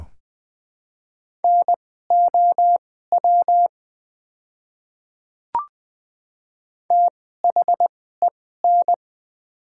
Morse code, heard as a keyed tone, about 20 wpm, spaced as for 14 wpm, 700 Hz. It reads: NOW THEN
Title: 2x Character Spacing - Top 200 Words - 20wpm